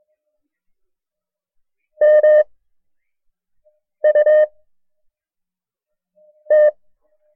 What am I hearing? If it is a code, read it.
Morse code, heard as a keyed tone, about 22 wpm, 610 Hz. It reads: M U T
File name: FreeFiles / CW-practice / KB6MT-tape1-side2